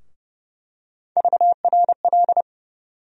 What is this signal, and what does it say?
Morse code, heard as a keyed tone, about 30 wpm, 700 Hz. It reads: VRL
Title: USA Call Sign Postfixes - 3 Letters 30wpm